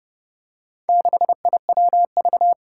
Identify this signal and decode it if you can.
Morse code, heard as a keyed tone, about 30 wpm, 700 Hz. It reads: 6IWV
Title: Four Challenging Letter-Number Combinations 30wpm